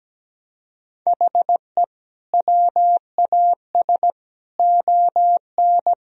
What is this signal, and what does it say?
Morse code, heard as a keyed tone, about 17 wpm, 700 Hz. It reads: HE WAS ON
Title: Sentences from Top 1000 Words - Review Part 3 - 17wpm